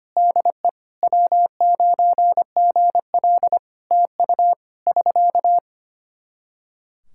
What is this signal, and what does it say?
Morse code, heard as a keyed tone, about 25 wpm, 700 Hz. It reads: DE W9GL TU <SK>
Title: K1USN SST QSO Practice 25wpm